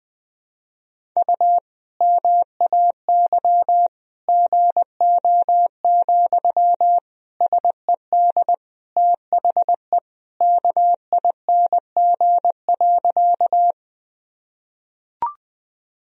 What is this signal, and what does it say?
Morse code, heard as a keyed tone, about 20 wpm, 700 Hz. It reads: U MAY GO, SED THE KING.
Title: Sentences from Top 400 Words and Abbreviations 20wpm